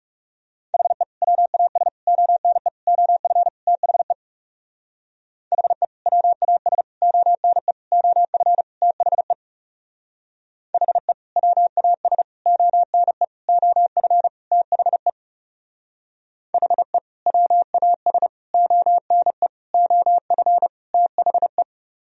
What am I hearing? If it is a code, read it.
Morse code, heard as a keyed tone, about 37 wpm, 700 Hz. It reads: HE WAS ONE OF THE HE WAS ONE OF THE HE WAS ONE OF THE HE WAS ONE OF THE
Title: Speed-Racing - Sets of 5 Words 30wpm